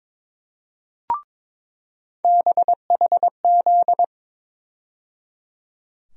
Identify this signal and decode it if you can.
Morse code, heard as a keyed tone, about 22 wpm, 700 Hz. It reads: BHZ
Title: USA Call Sign Postfixes - 3 Letters 22wpm